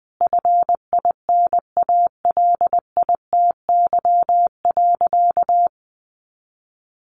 Morse code, heard as a keyed tone, about 20 wpm, 700 Hz. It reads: FINALITY.
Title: A Sound Of Thunder 20wpm